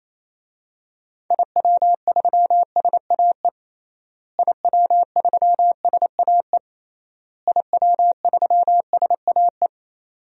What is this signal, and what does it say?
Morse code, heard as a keyed tone, about 28 wpm, 700 Hz. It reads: IW3SAE IW3SAE IW3SAE
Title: DXCC-100 Call Signs - 6 Characters - Repeated 3x - 28wpm